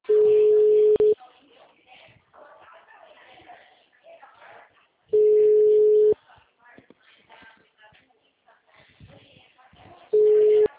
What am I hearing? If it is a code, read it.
Ringback tone (Europe, 425 Hz), with speech.
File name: out- -2002- - - .284